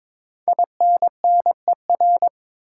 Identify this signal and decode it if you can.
Morse code, heard as a keyed tone, about 23 wpm, 700 Hz. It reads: INNER
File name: 5 Letter Words - First 3 Letters Given - 22wpm